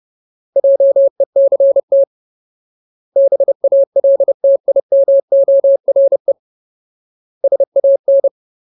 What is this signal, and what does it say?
Morse code, heard as a keyed tone, about 30 wpm, 550 Hz. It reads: JECT BALTIMORE SAN
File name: Ultimate Word List / 30WPM Ultimate